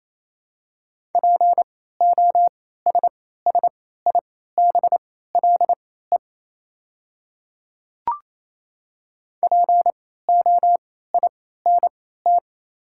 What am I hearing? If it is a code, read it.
Morse code, heard as a keyed tone, about 28 wpm, spaced as for 16 wpm, 700 Hz. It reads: POSSIBLE POINT
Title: 3x Character Spacing - Top 500 Words - 28wpm